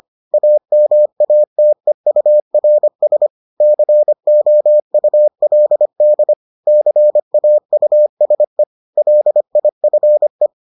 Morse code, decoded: AMATEURS COULD CAUSE LIFE

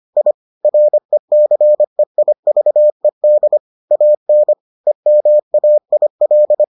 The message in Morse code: I RECEIVED AN EMAIL

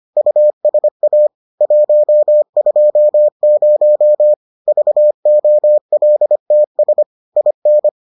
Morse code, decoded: USA 120 VOLTS IN

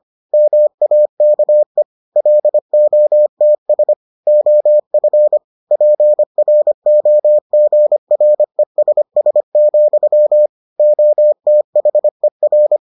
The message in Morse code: MAKE LOTS OF PROGRESS, OTHER

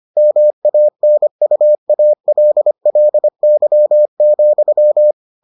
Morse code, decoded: MANUALLY,

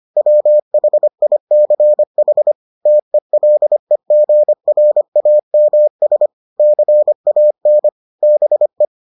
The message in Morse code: WHICH TELEGRAMS CAN BE

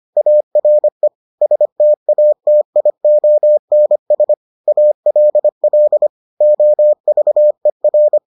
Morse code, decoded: ARE STATIONS ALL OVER

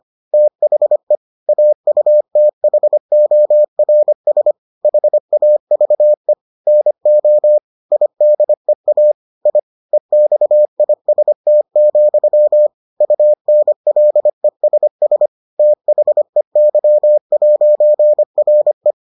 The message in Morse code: THE AUTHORS HAVE NO IDEA I EXIST, UNLESS THEY'RE

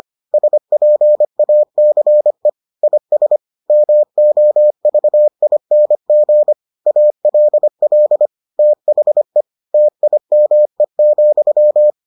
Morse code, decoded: SPACE IS MOVING ALL THE TIME,

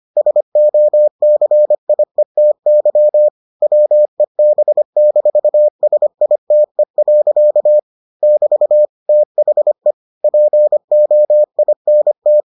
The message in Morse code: SOCIETY WEB-SITE. <BT> THE POINT

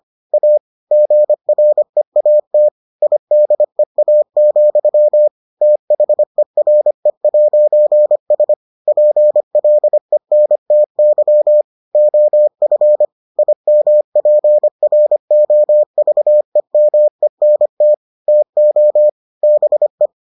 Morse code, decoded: A GREAT IDEA, THERE'S PLENTY OF IMPROVEMENT TO BE

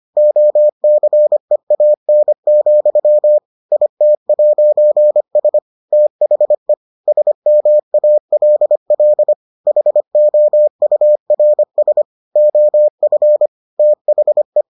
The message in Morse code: OCEAN, IT'S THE SMALL HOURS OF THE

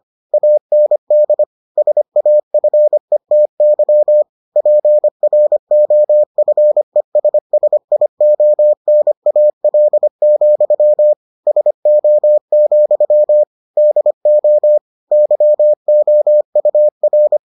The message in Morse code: AND SAFETY PROFESSIONAL, SO, DO YOUR

